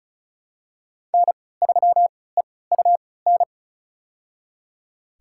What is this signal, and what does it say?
Morse code, heard as a keyed tone, about 36 wpm, spaced as for 20 wpm, 700 Hz. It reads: N3EUN
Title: USA Call Signs - 5 Characters - 3x Character Spacing - 35wpm